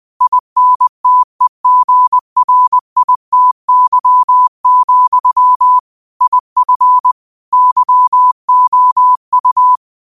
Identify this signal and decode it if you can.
Morse code, heard as a keyed tone, about 20 wpm, 1000 Hz. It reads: INTEGRITY, IF YOU
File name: 20wpm